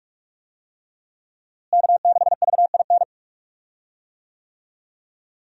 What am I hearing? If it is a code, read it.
Morse code, heard as a keyed tone, about 42 wpm, 700 Hz. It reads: K6VIN
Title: USA Call Signs - 5 Characters - Repeated 3x - 45wpm